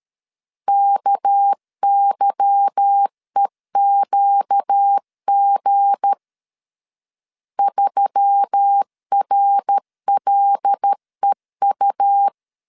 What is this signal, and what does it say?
Morse code, heard as a keyed tone, about 13 wpm, 800 Hz. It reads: KYEQG 3RLEU
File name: LEZIONI / 8 YJ45 / 8 Y 12wpm 2228